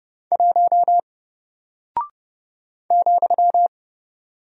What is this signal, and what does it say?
Morse code, heard as a keyed tone, about 30 wpm, 700 Hz. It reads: 1 ,
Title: Single Character (Letter:Number:Punctuation:Prosign) Alt 30wpm